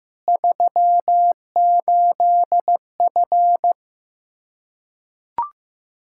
Morse code, heard as a keyed tone, about 16 wpm, 700 Hz. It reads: 38F
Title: Practice QSO - Exchange 2 - Info - 15wpm